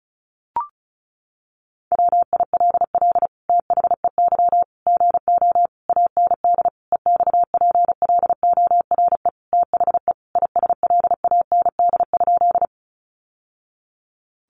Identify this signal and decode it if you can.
Morse code, heard as a keyed tone, about 34 wpm, 700 Hz. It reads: WILL THEY GO AND EXPLORE THE ISLAND?
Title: Sentences from Top 1100 Words 35wpm